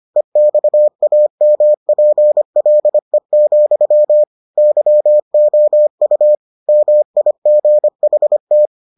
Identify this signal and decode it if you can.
Morse code, heard as a keyed tone, about 25 wpm, 600 Hz. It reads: EXAMPLE, YOU MIGHT